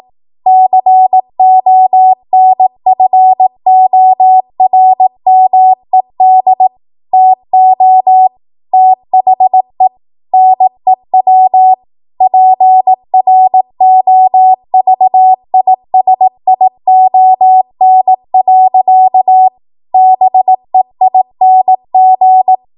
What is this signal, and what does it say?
Morse code, heard as a keyed tone, about 18 wpm, 750 Hz. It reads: CONFORMED TO THE NEW PROVISION. BEING
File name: Morse / 18WPM